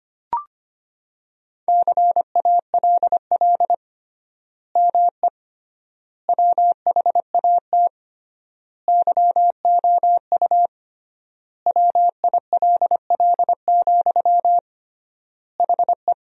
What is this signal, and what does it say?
Morse code, heard as a keyed tone, about 25 wpm, 700 Hz. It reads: CALL ME WHAT YOU WILL, HE
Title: 3x Word-Spacing - Sentences from Top 200 Words - 25wpm